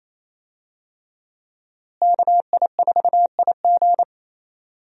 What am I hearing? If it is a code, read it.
Morse code, heard as a keyed tone, about 28 wpm, 700 Hz. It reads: KI4IG
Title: USA Call Signs - 5 Characters - Repeated 3x - 28wpm